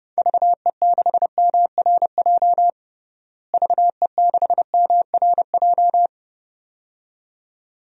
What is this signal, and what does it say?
Morse code, heard as a keyed tone, about 30 wpm, 700 Hz. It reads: VE6MRJ VE6MRJ
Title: DXCC-100 Call Signs - 6 Characters - Repeated 2x - 30wpm